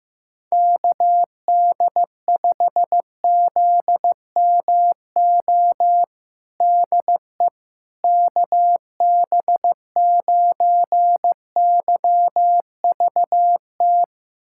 Morse code, heard as a keyed tone, about 15 wpm, 700 Hz. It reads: KD5ZMO DE KB9YVT
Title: USA Callsign DE Callsign - 6 Characters - 15wpm